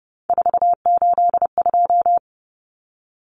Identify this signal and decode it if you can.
Morse code, heard as a keyed tone, about 30 wpm, 700 Hz. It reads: 482
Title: Common Cut Numbers - 3-Digit 30wpm